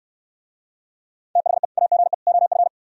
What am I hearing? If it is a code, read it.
Morse code, heard as a keyed tone, about 67 wpm, 700 Hz. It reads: THE AGE OF